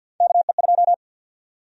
Morse code, heard as a keyed tone, about 52 wpm, 700 Hz. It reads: KE2
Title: USA Call Sign Prefixes - Repeated 2x - 50wpm